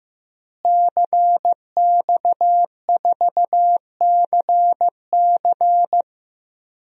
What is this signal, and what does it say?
Morse code, heard as a keyed tone, about 15 wpm, 700 Hz. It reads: CX4CC
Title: DXCC-100 Call Signs - 5 Characters - Repeated 2x - 15wpm